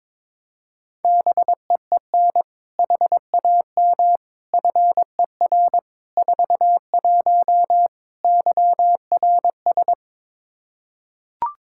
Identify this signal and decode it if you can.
Morse code, heard as a keyed tone, about 22 wpm, 700 Hz. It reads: BEEN HAM FER 41 YRS
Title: Practice QSO - Exchange 3 - More Info - 22wpm